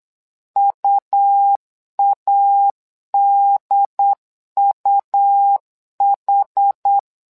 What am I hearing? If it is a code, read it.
Morse code, heard as a keyed tone, about 8 wpm, 800 Hz. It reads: UADUH
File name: LEZIONI / 2 UHDN / 2 UHD 8wpm 2258